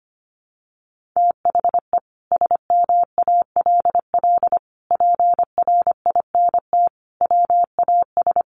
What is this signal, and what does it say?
Morse code, heard as a keyed tone, about 25 wpm, 700 Hz. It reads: THE SMALL PRINT WAS
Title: Sentences from Top 1400 Words 25wpm